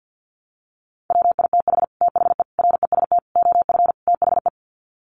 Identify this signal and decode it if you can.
Morse code, heard as a keyed tone, about 48 wpm, 700 Hz. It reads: WITH THE REST OF THE
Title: Sets of 5 Words 50wpm